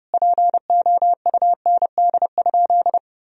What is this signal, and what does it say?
Morse code, heard as a keyed tone, about 30 wpm, 700 Hz. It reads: POUND?
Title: Sentences from Top 1000 Words - Review Part 2 - 30wpm